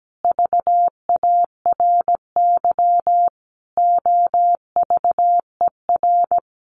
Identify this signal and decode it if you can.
Morse code, heard as a keyed tone, about 17 wpm, 700 Hz. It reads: VARY OVER
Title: Sentences from Top 1200 Words 17wpm